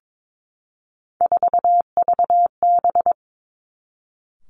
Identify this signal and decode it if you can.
Morse code, heard as a keyed tone, about 22 wpm, 700 Hz. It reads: 4VB
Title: Three Challenging Letter-Number Combinations 22wpm